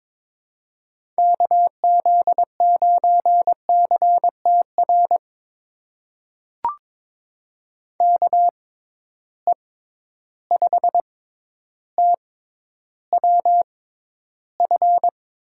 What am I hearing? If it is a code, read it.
Morse code, heard as a keyed tone, about 22 wpm, spaced as for 11 wpm, 700 Hz. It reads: KZ9CTR KE5TWF
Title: USA Call Signs - 6 Characters - 6x Character Spacing - 22wpm